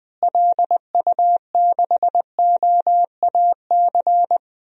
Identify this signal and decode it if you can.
Morse code, heard as a keyed tone, about 20 wpm, 700 Hz. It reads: LU6OAC
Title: DXCC-100 Call Signs - 6 Characters - Repeated 2x - 20wpm